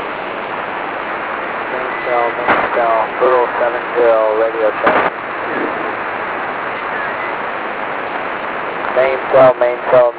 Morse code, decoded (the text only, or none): none